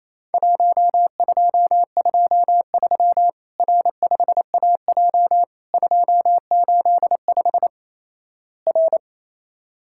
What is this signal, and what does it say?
Morse code, heard as a keyed tone, about 28 wpm, 700 Hz. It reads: 1223 R5AJ 285 R
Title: Worked All Europe DX Contest - QTC Exchange - 28wpm